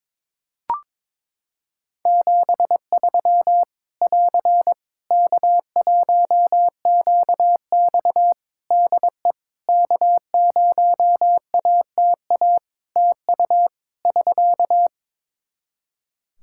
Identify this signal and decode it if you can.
Morse code, heard as a keyed tone, about 22 wpm, 700 Hz. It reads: 73 <AR> K1QX DE K0ATA TU <SK>
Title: Practice QSO - Exchange 4 - Ending - 22wpm